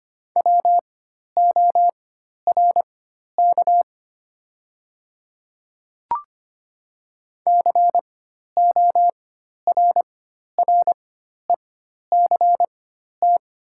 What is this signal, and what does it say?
Morse code, heard as a keyed tone, about 25 wpm, spaced as for 12 wpm, 690 Hz. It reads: WORK CORRECT
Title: 4x Character Spacing - Top 500 Words - 25wpm